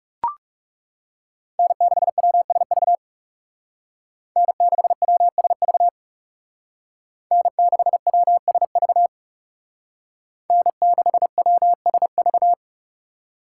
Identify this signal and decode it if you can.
Morse code, heard as a keyed tone, about 35 wpm, 700 Hz. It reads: N6WSV N6WSV N6WSV N6WSV
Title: Speed Racing - USA 1x3 Call Signs 30wpm